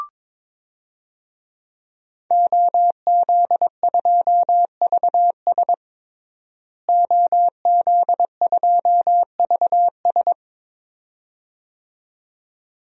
Morse code, heard as a keyed tone, about 22 wpm, 700 Hz. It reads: OZ2VS OZ2VS
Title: DXCC-100 Call Signs - 5 Characters - Repeated 2x - 22wpm